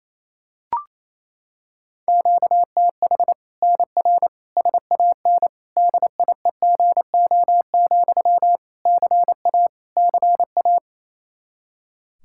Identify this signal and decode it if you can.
Morse code, heard as a keyed tone, about 28 wpm, 700 Hz. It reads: QTH NR SAN DIEGO, CA CA
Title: Practice QSO - Exchange 1 - Beginning - 28wpm